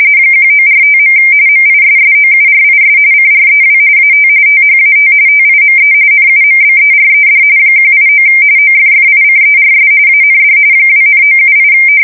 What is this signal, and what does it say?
RTTY, about 45.45 baud. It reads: 1945. MISSOURIS CREW AT THAT TIME NUMBERED 189 OFFICERS AND 2,978
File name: fsk170 / USS-Missouri